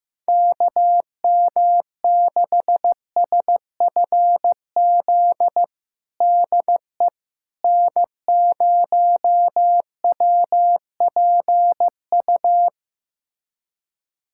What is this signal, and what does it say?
Morse code, heard as a keyed tone, about 15 wpm, 700 Hz. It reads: KM6SFZ DE N0WPU
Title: USA Callsign DE Callsign 15wpm